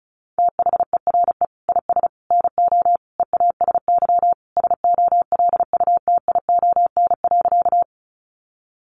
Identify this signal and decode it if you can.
Morse code, heard as a keyed tone, about 34 wpm, 700 Hz. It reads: THERE IS NO EASY SOLUTION.
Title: Sentences from Top 1300 Words 35wpm